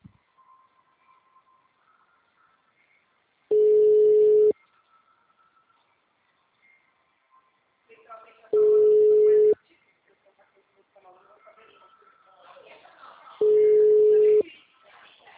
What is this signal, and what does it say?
Ringback tone (Europe, 425 Hz), with speech.
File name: out- -2012- - - .38565